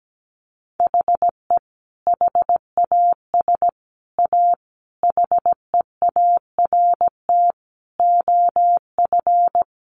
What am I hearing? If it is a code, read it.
Morse code, heard as a keyed tone, about 17 wpm, 700 Hz. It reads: HE HAS A HEART OF
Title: Sentences from Top 1200 Words 17wpm